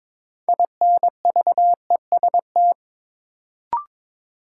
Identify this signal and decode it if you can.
Morse code, heard as a keyed tone, about 22 wpm, 700 Hz. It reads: INVEST
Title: Sentences from Top 1100 Words - New Word Drill Alt - 22wpm